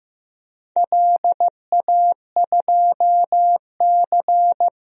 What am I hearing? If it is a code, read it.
Morse code, heard as a keyed tone, about 15 wpm, 700 Hz. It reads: LA2C
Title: DXCC-100 Call Signs - 4 Characters - 15wpm